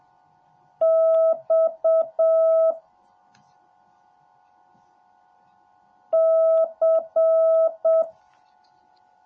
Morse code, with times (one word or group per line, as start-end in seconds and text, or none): 0.8-2.7 X
6.1-8.0 C